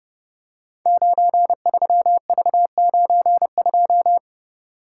Morse code, heard as a keyed tone, about 30 wpm, 700 Hz. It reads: 93V92
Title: Code Groups - 5 Letter:Numbers - 30wpm